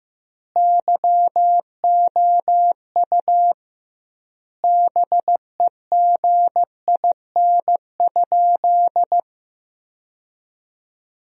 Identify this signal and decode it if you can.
Morse code, heard as a keyed tone, about 15 wpm, 700 Hz. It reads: YOU BEGIN?
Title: 2x Word-Spacing - Sentences from Top 300 Words - 15wpm